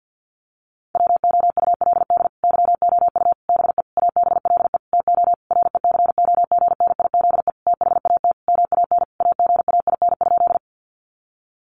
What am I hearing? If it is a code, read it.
Morse code, heard as a keyed tone, about 50 wpm, 700 Hz. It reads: WOULD YOU BE ABLE TO RECOGNIZE THAT MAN AGAIN?
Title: Sentences from Top 1400 Words 50wpm